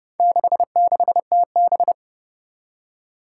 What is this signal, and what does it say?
Morse code, heard as a keyed tone, about 29 wpm, 700 Hz. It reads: 66TB
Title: Four Challenging Letter-Number Combinations 30wpm